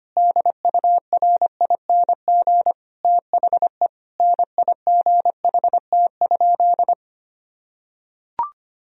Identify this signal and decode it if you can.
Morse code, heard as a keyed tone, about 25 wpm, 700 Hz. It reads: DURING THE NIGHT?
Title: Sentences from Top 1000 Words - Review Part 2 - 25wpm